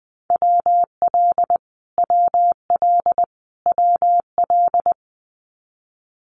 Morse code, heard as a keyed tone, about 20 wpm, 700 Hz. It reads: WL WL WL
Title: Abbreviations Repeated 3x - 20wpm